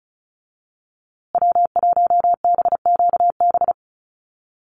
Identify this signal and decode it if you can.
Morse code, heard as a keyed tone, about 34 wpm, 700 Hz. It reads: W1BQB
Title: USA Call Signs Repeated 2x 35wpm